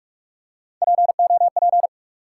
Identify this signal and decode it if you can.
Morse code, heard as a keyed tone, about 44 wpm, 700 Hz. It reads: POP
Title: Three Letter Words - 45wpm